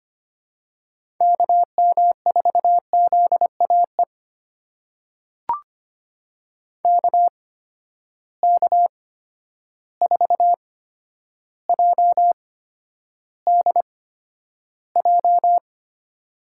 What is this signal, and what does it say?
Morse code, heard as a keyed tone, about 25 wpm, spaced as for 11 wpm, 700 Hz. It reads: KM4ZAE KK4JDJ
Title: USA Call Signs - 8x Character Spacing - 25wpm